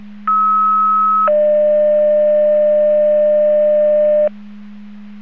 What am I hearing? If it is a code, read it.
Two-tone page, heard as a one-second tone, then a three-second tone, about 1290 then 600 Hz. It is heard from 0.3 to 4.3 s.